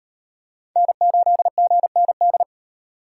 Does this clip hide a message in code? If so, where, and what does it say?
Yes: Morse code from 0.8 to 2.4 s, heard as a keyed tone, about 38 wpm, 700 Hz. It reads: N8GND